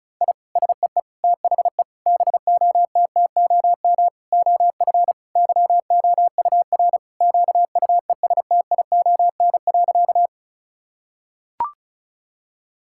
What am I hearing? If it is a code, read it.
Morse code, heard as a keyed tone, about 34 wpm, 700 Hz. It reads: I SEE THE BOTTOM OF YOUR QUESTION.
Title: Sentences from Top 1000 Words - Review Part 6 - 35wpm